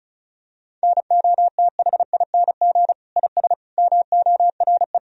Morse code, heard as a keyed tone, about 36 wpm, 700 Hz. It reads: NOTHING IS MORE
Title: Sentences from Top 1000 Words - Review Part 4 - 35wpm